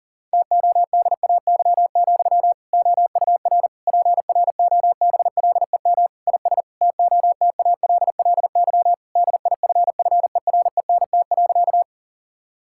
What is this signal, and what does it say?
Morse code, heard as a keyed tone, about 40 wpm, 700 Hz. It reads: TODAY, OUR PROBLEM IS TOTALLY DIFFERENT.
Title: Sentences from Top 1600 Words 40wpm